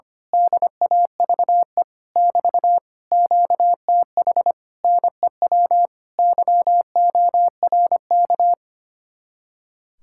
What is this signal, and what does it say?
Morse code, heard as a keyed tone, about 24 wpm, 700 Hz. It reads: DAVE <BT> QTH NEW YORK
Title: Speed-Racing Common QSO Phrases Encore - 25wpm